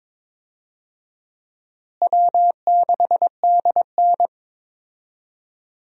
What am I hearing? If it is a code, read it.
Morse code, heard as a keyed tone, about 22 wpm, 700 Hz. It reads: W6DN
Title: USA Call Signs - 4 Characters - Repeated 2x - 22wpm